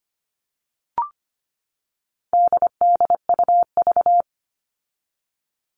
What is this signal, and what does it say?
Morse code, heard as a keyed tone, about 25 wpm, 700 Hz. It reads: DDUV
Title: All Cut Numbers - 4-Digit 25wpm